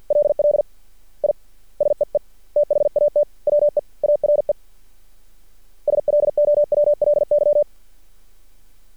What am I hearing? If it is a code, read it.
Morse code, heard as a keyed tone, about 50 wpm, 590 Hz. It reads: LL I SEE THAT WE ARE SLOWLY
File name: archived / CRSnet